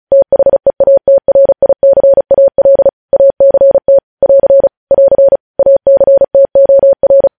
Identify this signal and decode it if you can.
Morse code, heard as a keyed tone, about 36 wpm, 550 Hz. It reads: THEATRICAL ACT <AR> <AR> ACTOR
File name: podcast / News headlines-1x-35x35